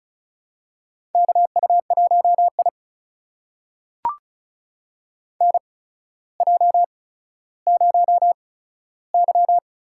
Morse code, heard as a keyed tone, about 36 wpm, spaced as for 15 wpm, 700 Hz. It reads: KU1I NJ0Y
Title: USA Call Signs - 4 Characters - 8x Character Spacing - 35wpm